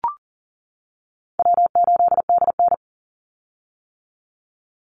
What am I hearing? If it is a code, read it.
Morse code, heard as a keyed tone, about 40 wpm, 700 Hz. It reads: W8DN
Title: Call Signs - 40wpm